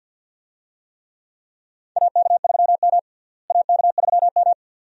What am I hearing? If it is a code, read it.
Morse code, heard as a keyed tone, about 52 wpm, 700 Hz. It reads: AK3M AK3M
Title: USA Call Signs - 4 Characters - Repeated 2x - 50wpm